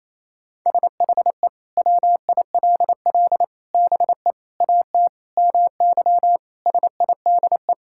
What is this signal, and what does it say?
Morse code, heard as a keyed tone, about 28 wpm, 700 Hz. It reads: SHE WILL BE AT MY SIDE
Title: Sentences from Top 1600 Words 28wpm